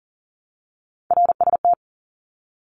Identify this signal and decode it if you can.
Morse code, heard as a keyed tone, about 40 wpm, 700 Hz. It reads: RST
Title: Abbreviations 40wpm